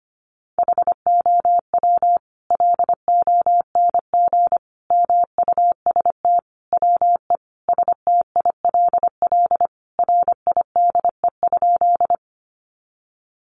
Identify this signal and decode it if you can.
Morse code, heard as a keyed tone, about 25 wpm, 700 Hz. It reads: HOW LONG MUST WE STILL RIDE?
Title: Sentences from Top 1200 Words 25wpm